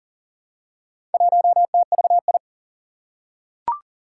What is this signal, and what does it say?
Morse code, heard as a keyed tone, about 41 wpm, 700 Hz. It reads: 1TVI
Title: Four Challenging Letter-Number Combinations 40wpm